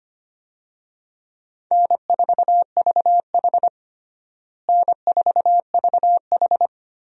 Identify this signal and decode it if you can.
Morse code, heard as a keyed tone, about 25 wpm, 700 Hz. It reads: N4VH N4VH
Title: USA Call Signs - 4 Characters - Repeated 3x - 25wpm